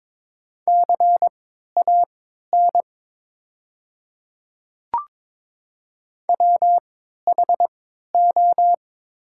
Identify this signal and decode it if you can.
Morse code, heard as a keyed tone, about 22 wpm, spaced as for 13 wpm, 700 Hz. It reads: CAN WHO
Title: 3x Character Spacing - Top 500 Words - 22wpm